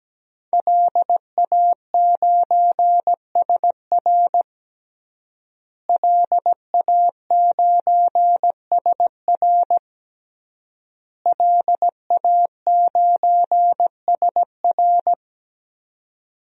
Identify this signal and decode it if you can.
Morse code, heard as a keyed tone, about 17 wpm, 700 Hz. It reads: LA9SR LA9SR LA9SR
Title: DXCC-100 Call Signs - 5 Characters - Repeated 3x - 17wpm